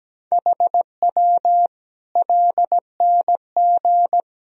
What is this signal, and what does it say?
Morse code, heard as a keyed tone, about 17 wpm, 700 Hz. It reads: HW LNG
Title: Sentences from Top 500 Words and Abbreviations 17wpm